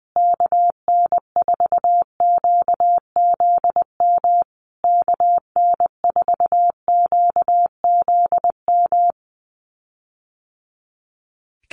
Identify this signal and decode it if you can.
Morse code, heard as a keyed tone, about 20 wpm, 700 Hz. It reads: KN4QZM KN4QZM
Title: USA Call Signs Repeated 3x 20wpm